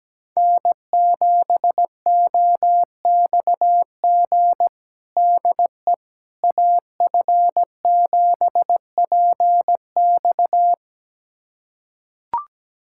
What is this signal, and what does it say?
Morse code, heard as a keyed tone, about 17 wpm, 700 Hz. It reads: N7OXG DE AF7PX
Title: USA Callsign DE Callsign - 5 Characters - 17wpm